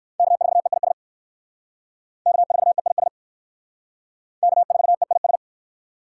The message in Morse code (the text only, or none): K4EIS K4EIS K4EIS